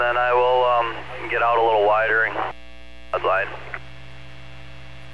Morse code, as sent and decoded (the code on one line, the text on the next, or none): none